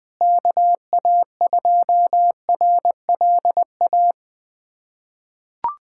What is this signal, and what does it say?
Morse code, heard as a keyed tone, about 20 wpm, 700 Hz. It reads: KA2RLA
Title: USA Callsign DE Callsign 20wpm